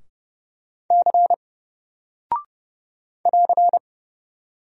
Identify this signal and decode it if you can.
Morse code, heard as a keyed tone, about 30 wpm, 700 Hz. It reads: C <AR>
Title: Single Character (Letter:Number:Punctuation:Prosign) Alt 30wpm